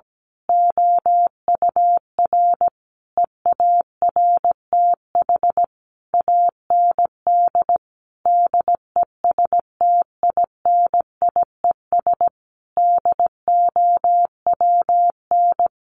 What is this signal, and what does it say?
Morse code, heard as a keyed tone, about 17 wpm, 700 Hz. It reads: OUR EARTH AND DESTINIES DOWN
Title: A Sound Of Thunder 17wpm